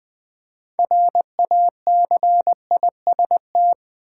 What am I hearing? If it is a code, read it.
Morse code, heard as a keyed tone, about 20 wpm, 700 Hz. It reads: RACIST
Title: Build A Word - 3 Letters At A Time - 6-Letter Words - 20wpm